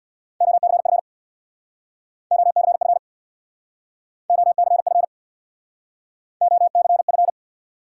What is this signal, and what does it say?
Morse code, heard as a keyed tone, about 66 wpm, 700 Hz. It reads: OCF OCF OCF OCF